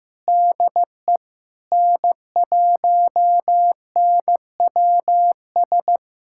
Morse code, heard as a keyed tone, about 15 wpm, 700 Hz. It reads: DE N1NWS
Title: USA Callsign DE Callsign - 5 Characters - 15wpm